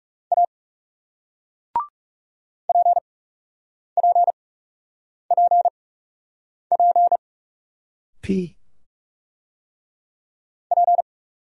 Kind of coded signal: Morse code

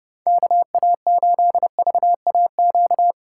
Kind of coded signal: Morse code

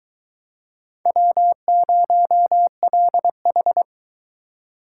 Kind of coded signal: Morse code

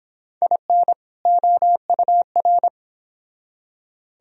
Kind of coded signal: Morse code

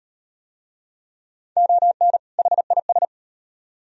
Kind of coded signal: Morse code

